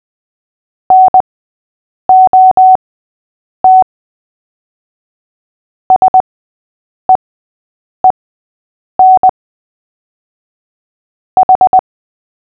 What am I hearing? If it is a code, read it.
Morse code, heard as a keyed tone, about 20 wpm, spaced as for 8 wpm, 750 Hz. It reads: NOT SEEN H
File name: cw / fists / Short-Story-5-8wpm